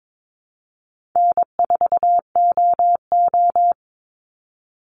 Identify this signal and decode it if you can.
Morse code, heard as a keyed tone, about 22 wpm, 690 Hz. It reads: N4OO
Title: Call Signs Repeated 2x - 22wpm